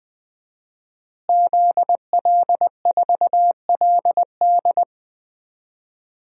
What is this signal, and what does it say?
Morse code, heard as a keyed tone, about 20 wpm, 700 Hz. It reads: ZL4LD
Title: DXCC-100 Call Signs - 5 Characters - Repeated 3x - 20wpm